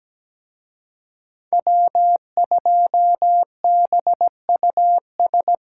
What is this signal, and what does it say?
Morse code, heard as a keyed tone, about 20 wpm, 700 Hz. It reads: W2BUS
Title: Speed Racing - USA 1x3 Call Signs 17wpm